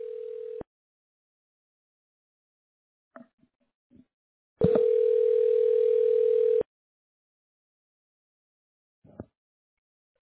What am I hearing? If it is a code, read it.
Ringback tone (North America), 440 + 480 Hz.